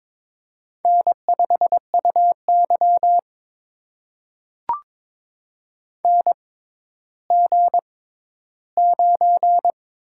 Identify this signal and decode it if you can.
Morse code, heard as a keyed tone, about 22 wpm, spaced as for 13 wpm, 700 Hz. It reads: N5UY NG9
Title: USA Call Signs - 4 Characters - 6x Character Spacing - 22wpm